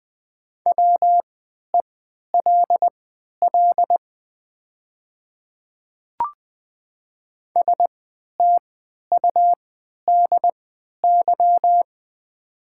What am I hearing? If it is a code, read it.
Morse code, heard as a keyed tone, about 20 wpm, spaced as for 11 wpm, 700 Hz. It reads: WELL STUDY
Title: 3x Character Spacing - Top 300 Words - 20wpm